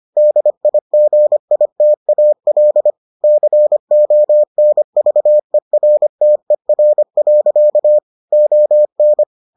Morse code, heard as a keyed tone, about 25 wpm, 600 Hz. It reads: DIGITAL CONVERTER. ON